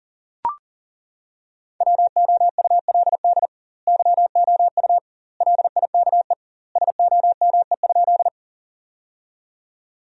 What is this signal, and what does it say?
Morse code, heard as a keyed tone, about 40 wpm, 700 Hz. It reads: WOULD YOU LIKE SOME?
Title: 2x Word-Spacing - Sentences from Top 100 Words - 40wpm